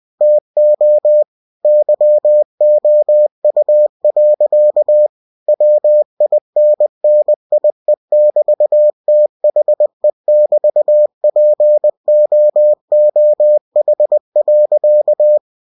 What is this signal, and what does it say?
Morse code, heard as a keyed tone, about 20 wpm, 600 Hz. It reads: TO YOU. WINNIE<BT>THE<BT>POOH.
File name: cw-books / winnie-the-pooh / winnie-20wpm-page-0000